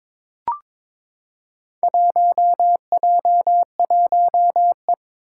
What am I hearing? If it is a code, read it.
Morse code, heard as a keyed tone, about 22 wpm, 700 Hz. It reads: 1J1E
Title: Four Challenging Letter-Number Combinations 22wpm